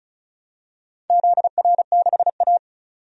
Morse code, heard as a keyed tone, about 36 wpm, 690 Hz. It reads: ZR6A
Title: DXCC-100 Call Signs - 4 Characters - 35wpm